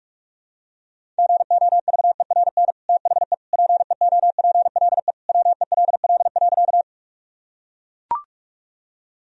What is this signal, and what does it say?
Morse code, heard as a keyed tone, about 42 wpm, 700 Hz. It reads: GOVERN THE PEOPLE WELL.
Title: Sentences from Top 1000 Words - Review Part 2 - 45wpm